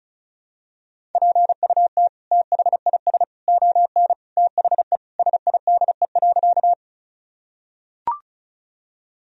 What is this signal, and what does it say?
Morse code, heard as a keyed tone, about 36 wpm, 700 Hz. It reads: PUT THIS ON THE SIDE.
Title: Sentences from Top 1000 Words - Review Part 1 - 35wpm